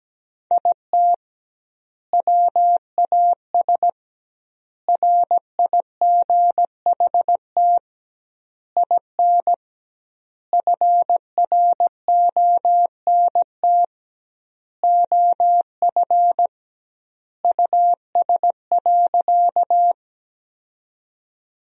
Morse code, heard as a keyed tone, about 17 wpm, 700 Hz. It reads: IT WAS RIGHT IN FRONT OF US.
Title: 2x Word-Spacing - Sentences from Top 500 Words - 17wpm